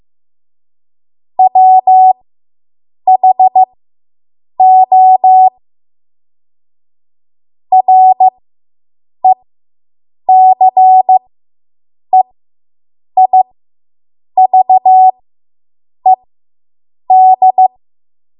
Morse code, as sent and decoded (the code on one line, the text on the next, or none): .-- .... --- / .-. -.-. .. ...- . -..
WHO RECEIVED